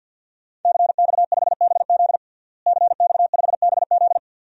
Morse code, heard as a keyed tone, about 52 wpm, 700 Hz. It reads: CX5BZ CX5BZ
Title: DXCC-100 Call Signs - 5 Characters - Repeated 2x - 50wpm